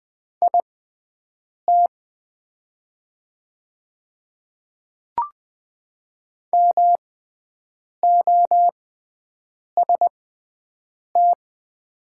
Morse code, heard as a keyed tone, about 20 wpm, spaced as for 7 wpm, 700 Hz. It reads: IT MOST
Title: 6x Character Spacing - Top 100 Words - 20wpm